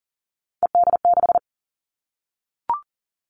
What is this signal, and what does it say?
Morse code, heard as a keyed tone, about 38 wpm, 700 Hz. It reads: ED6
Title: Callsign Prefixes for Top 50 DXCC - 40wpm